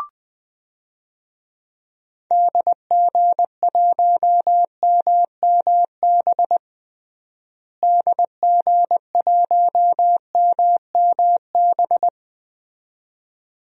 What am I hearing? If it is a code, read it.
Morse code, heard as a keyed tone, about 21 wpm, 700 Hz. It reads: DG1MMB DG1MMB
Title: DXCC-100 Call Signs - 6 Characters - Repeated 2x - 20wpm